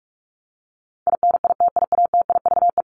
Morse code, heard as a keyed tone, about 44 wpm, 700 Hz. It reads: INITIATIVE
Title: Top 1751-2000 Words 45wpm